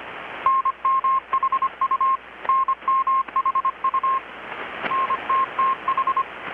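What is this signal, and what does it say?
Morse code, heard as a keyed tone, about 25 wpm, 1070 Hz, with radio noise. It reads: NMHU NMHU NTTH
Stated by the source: Morse from KPH's Night of Nights XX maritime radiotelegraphy event, received in Chiapas, Mexico about 3,800 km away on a Kenwood R-600 with a grounded 10 m long-wire antenna.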